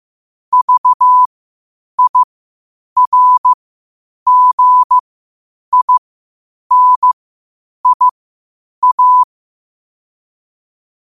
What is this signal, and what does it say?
Morse code, heard as a keyed tone, about 15 wpm, spaced as for 8 wpm, 1000 Hz. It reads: VIRGINIA
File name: qotdfiles / 8wpm